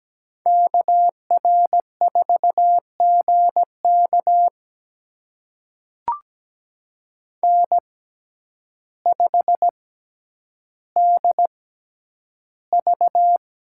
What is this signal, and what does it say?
Morse code, heard as a keyed tone, about 17 wpm, spaced as for 9 wpm, 700 Hz. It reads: KR4GK N5DV
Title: USA Call Signs - 5 Characters - 6x Character Spacing - 17wpm